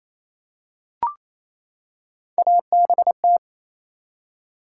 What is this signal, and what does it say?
Morse code, heard as a keyed tone, about 28 wpm, 700 Hz. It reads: ABT
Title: Top 90 QSO Elements 28wpm